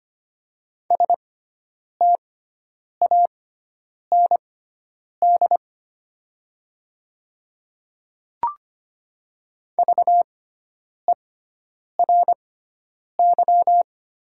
Morse code, heard as a keyed tone, about 25 wpm, spaced as for 9 wpm, 700 Hz. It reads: STAND VERY
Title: 6x Character Spacing - Top 300 Words - 25wpm